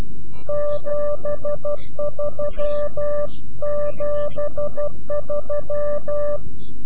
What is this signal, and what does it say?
Morse code, heard as a keyed tone, about 12 wpm, 590 Hz. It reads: 73 73
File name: CW-QSO / CQ Call / CQ-QSO